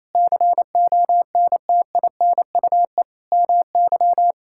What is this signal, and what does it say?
Morse code, heard as a keyed tone, about 27 wpm, 700 Hz. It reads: CONTINUE MY
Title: Sentences from Top 1000 Words - Review Part 6 - 28wpm